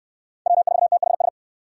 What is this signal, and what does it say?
Morse code, heard as a keyed tone, about 64 wpm, 700 Hz. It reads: W4THS